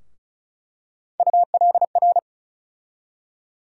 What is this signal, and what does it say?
Morse code, heard as a keyed tone, about 36 wpm, 700 Hz. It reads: ULR
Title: USA Call Sign Postfixes - 3 Letters 35wpm